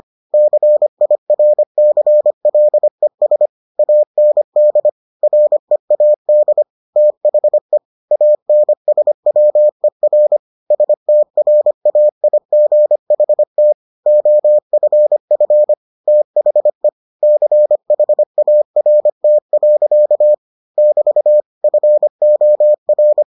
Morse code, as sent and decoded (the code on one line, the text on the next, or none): -.-. .. .-. -.-. .-.. ... / .- -. -.. / .-. .- -.. / - .... / .- -. ... .-- . .-. / ... - .-. .- .. --. .... - / --- ..-. ..-. / - .... / -.-. .... .- .-. - .-.-.- / -...- / ..-. --- .-.
CIRCLES AND READ THE ANSWER STRAIGHT OFF THE CHART. <BT> FOR